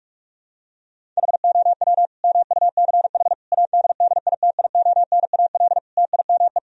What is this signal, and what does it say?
Morse code, heard as a keyed tone, about 43 wpm, 700 Hz. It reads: HOW MUCH ADDITIONAL TIME
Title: Sentences from Top 1000 Words - Review Part 4 - 45wpm